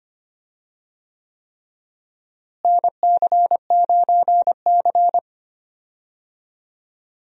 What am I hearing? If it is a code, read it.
Morse code, heard as a keyed tone, about 25 wpm, 700 Hz. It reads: NC9C
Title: USA Call Signs - 4 Characters - Repeated 3x - 25wpm